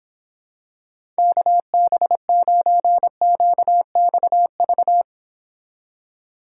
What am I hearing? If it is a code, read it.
Morse code, heard as a keyed tone, about 26 wpm, 700 Hz. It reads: KB9QXV